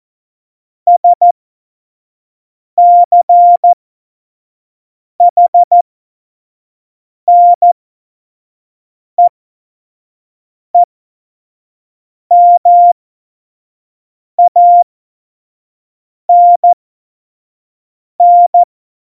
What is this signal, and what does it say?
Morse code, heard as a keyed tone, about 14 wpm, spaced as for 5 wpm, 700 Hz. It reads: SCHNEEMANN
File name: Morsekurs / Witze / Witz 24. Oktober 17, Tempo 25